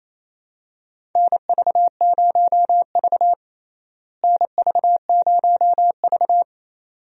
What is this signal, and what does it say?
Morse code, heard as a keyed tone, about 27 wpm, 700 Hz. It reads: NV0V NV0V
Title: USA Call Signs - 4 Characters - Repeated 2x - 28wpm